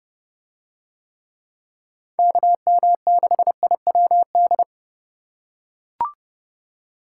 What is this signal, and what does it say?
Morse code, heard as a keyed tone, about 30 wpm, 700 Hz. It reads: KM6IWD